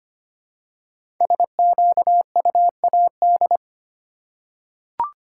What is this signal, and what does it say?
Morse code, heard as a keyed tone, about 25 wpm, 700 Hz. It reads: SQUAD
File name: 5 Letter Words - First 3 Letters Given - 25wpm